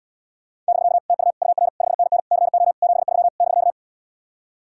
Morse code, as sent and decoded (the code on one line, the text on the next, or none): -...-.- / - ..- / ..- .-. / ..... -. -. / -.-. --- / --... ...-- / -...-.-
<BK> TU UR 5NN CO 73 <BK>